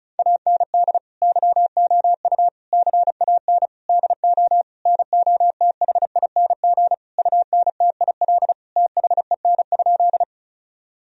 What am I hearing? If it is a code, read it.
Morse code, heard as a keyed tone, about 36 wpm, 700 Hz. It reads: AND YOU CAN DO NOTHING UNTIL THEN?
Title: Sentences from Top 1000 Words - Review Part 1 - 35wpm